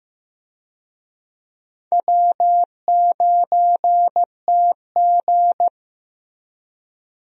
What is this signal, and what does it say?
Morse code, heard as a keyed tone, about 15 wpm, 700 Hz. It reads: W9TG
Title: USA Call Signs - 4 Characters - Repeated 2x - 15wpm